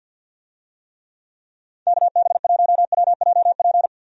Morse code, heard as a keyed tone, about 52 wpm, 700 Hz. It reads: KB1WJP
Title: USA Call Signs - 6 Characters - 50wpm